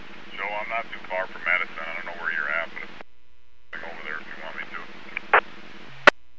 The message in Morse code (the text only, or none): none